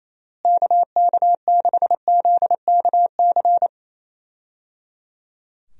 Morse code, heard as a keyed tone, about 28 wpm, 700 Hz. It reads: KK6ZKC
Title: USA Call Signs - 6 Characters - Repeated 3x - 28wpm